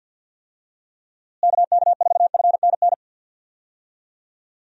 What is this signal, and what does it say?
Morse code, heard as a keyed tone, about 53 wpm, 700 Hz. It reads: KK4FNN
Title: USA Call Signs - 6 Characters - Repeated 2x - 50wpm